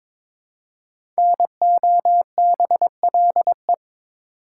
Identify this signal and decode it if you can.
Morse code, heard as a keyed tone, about 22 wpm, 700 Hz. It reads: NOBLE
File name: 5 Letter Words - First 3 Letters Given - 22wpm